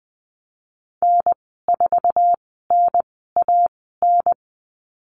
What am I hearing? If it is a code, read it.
Morse code, heard as a keyed tone, about 20 wpm, spaced as for 14 wpm, 700 Hz. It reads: N4NAN
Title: 082 Lesson 14g - Number 2 - Call Signs 20wpm